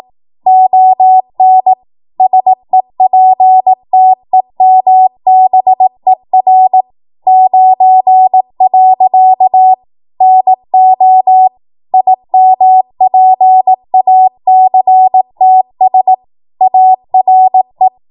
Morse code, heard as a keyed tone, about 18 wpm, 750 Hz. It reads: ON SEPTEMBER 9. NO IMPACTS ARE
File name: Morse / 18WPM